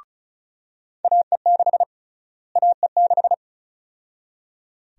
Morse code, heard as a keyed tone, about 34 wpm, 700 Hz. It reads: AE6 AE6
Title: USA Call Sign Prefixes - Repeated 2x - 35wpm